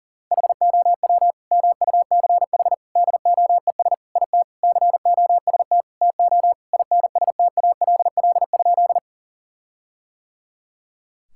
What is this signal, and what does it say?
Morse code, heard as a keyed tone, about 40 wpm, 700 Hz. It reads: HOW MUCH DOES IT COST TO INSTALL?
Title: Sentences from Top 1600 Words 40wpm